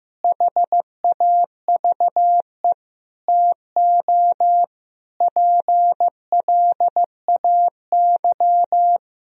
Morse code, heard as a keyed tone, about 15 wpm, 700 Hz. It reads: HAVE TO PLAY
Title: Sentences from Top 1000 Words - Review Part 1 - 15wpm